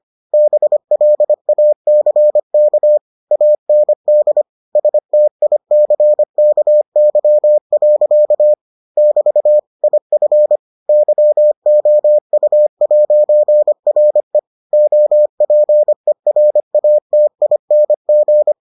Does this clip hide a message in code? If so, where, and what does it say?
Yes: Morse code from 0.3 to 18.5 s, heard as a keyed tone, about 25 wpm, 600 Hz. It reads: BLACK AND STICKY. <BT> IF YOU'RE OPERATING